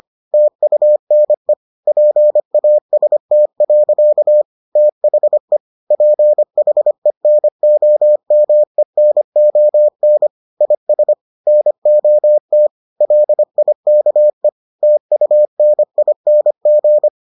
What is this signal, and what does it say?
Morse code, heard as a keyed tone, about 25 wpm, 600 Hz. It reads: TUNE PAST. THE PHENOMENON IS NOT LIKE TUNING